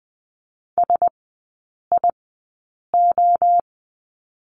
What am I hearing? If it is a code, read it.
Morse code, heard as a keyed tone, about 20 wpm, spaced as for 8 wpm, 700 Hz. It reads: SIO
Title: 081 Lesson 14e - Number 2 - Three Characters 20wpm